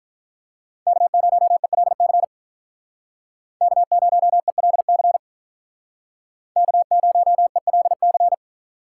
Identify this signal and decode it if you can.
Morse code, heard as a keyed tone, about 43 wpm, 700 Hz. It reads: K0ELC K0ELC K0ELC
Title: Speed Racing - USA Call Signs 35wpm